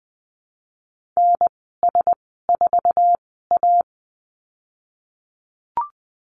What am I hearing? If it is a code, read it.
Morse code, heard as a keyed tone, about 19 wpm, spaced as for 14 wpm, 700 Hz. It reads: NS4A
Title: 055 Lesson 10g - Letter r - Call Signs 20wpm